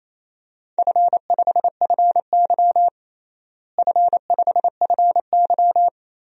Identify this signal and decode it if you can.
Morse code, heard as a keyed tone, about 28 wpm, 700 Hz. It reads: F5FY F5FY
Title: DXCC-100 Call Signs - 4 Characters - Repeated 2x - 28wpm